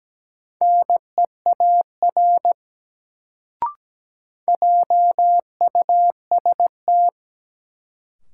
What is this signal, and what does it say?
Morse code, heard as a keyed tone, about 17 wpm, 700 Hz. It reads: NEAR JUST
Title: Sentences from Top 200 Words - New Word Drill - 17wpm